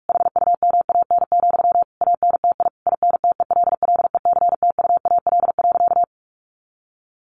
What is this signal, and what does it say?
Morse code, heard as a keyed tone, about 44 wpm, 700 Hz. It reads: HUMAN, ANTI INTELLECTUAL.
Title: A Sound Of Thunder 45wpm